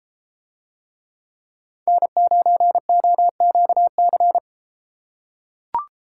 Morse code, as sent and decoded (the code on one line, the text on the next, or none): -. ----. --- --.- -.-.
N9OQC